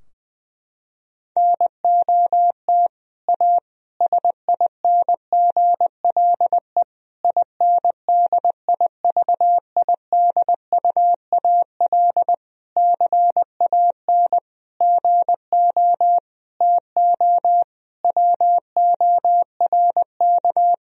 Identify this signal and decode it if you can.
Morse code, heard as a keyed tone, about 21 wpm, 700 Hz. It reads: NOT A SINGLE INDIVIDUAL CAN GO TO WORK
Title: Sentences from Top 1000 Words - Review Part 4 - 20wpm